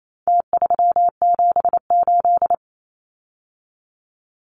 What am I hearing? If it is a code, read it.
Morse code, heard as a keyed tone, about 28 wpm, 700 Hz. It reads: T378
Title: Numbers - Contest Style - 4-Digit 28wpm